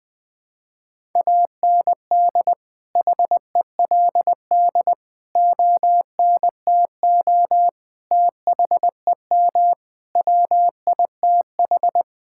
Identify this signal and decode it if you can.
Morse code, heard as a keyed tone, about 20 wpm, 700 Hz. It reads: AND HELD ONTO THEM WITH
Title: Sets of 9 Words - 20wpm